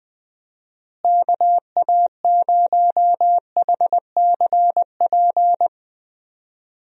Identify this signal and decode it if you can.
Morse code, heard as a keyed tone, about 20 wpm, 700 Hz. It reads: KA0HCP
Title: USA Call Signs - 6 Characters - Repeated 3x - 20wpm